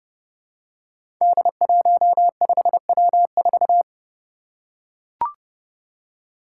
Morse code, heard as a keyed tone, about 30 wpm, 700 Hz. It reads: D15W4
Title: Five Challenging Letter-Number Combinations 30wpm